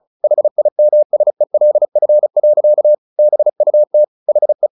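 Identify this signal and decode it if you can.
Morse code, heard as a keyed tone, about 34 wpm, 600 Hz. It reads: HIMSELF. BUT HE
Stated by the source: Morse practice at 35 wpm from CW med Gnister, sending Ken Follett's novel Never in English.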